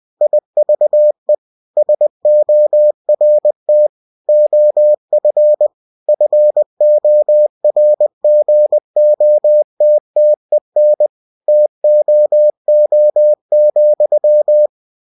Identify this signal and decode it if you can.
Morse code, heard as a keyed tone, about 20 wpm, 600 Hz. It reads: IVE SORT OF FORGOTTEN TOO,
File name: cw-books / winnie-the-pooh / winnie-20wpm-page-0084